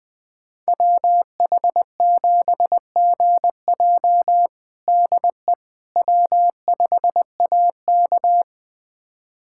Morse code, heard as a keyed tone, about 20 wpm, 700 Hz. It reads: WH7GJ DE W5AK
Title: USA Callsign DE Callsign 20wpm